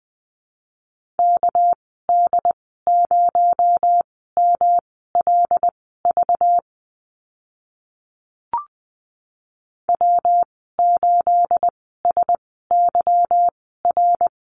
Morse code, heard as a keyed tone, about 20 wpm, spaced as for 14 wpm, 700 Hz. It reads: KD0MLV W8SYR
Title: 238 Lesson 38g - Letter q - Call Signs 20wpm